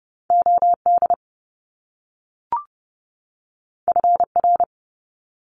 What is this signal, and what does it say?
Morse code, heard as a keyed tone, about 30 wpm, 700 Hz. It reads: OD FR
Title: Top Two Letter Combinations in Words 30wpm